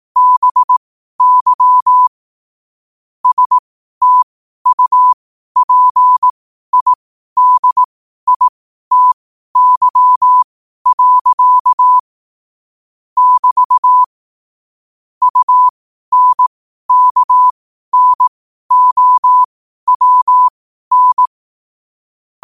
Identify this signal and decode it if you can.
Morse code, heard as a keyed tone, about 18 wpm, spaced as for 12 wpm, 1000 Hz. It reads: BY STUPIDITY. <BT> UNKNOWN
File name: qotdfiles / 12wpm